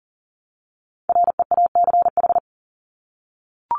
Morse code, heard as a keyed tone, about 40 wpm, 700 Hz. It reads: REACH
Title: Top 500 Words Repeated 40wpm